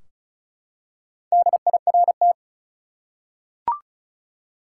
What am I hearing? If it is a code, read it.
Morse code, heard as a keyed tone, about 36 wpm, 700 Hz. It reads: DIRT
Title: Four Letter Words - 35wpm